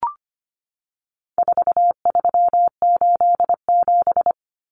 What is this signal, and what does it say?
Morse code, heard as a keyed tone, about 25 wpm, 700 Hz. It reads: 4387
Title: Numbers - 4-Digit 25wpm